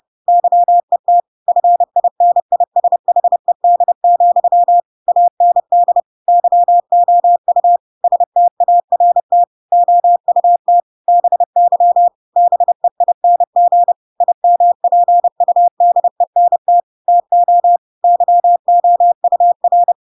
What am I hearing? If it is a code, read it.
Morse code, heard as a keyed tone, about 30 wpm, 700 Hz. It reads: YET FINISHED, AND YOU START OUT BY BEING IMPUDENT TO YOUR
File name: Pinocchio / Pinocchio ch3 30 30 ik1hge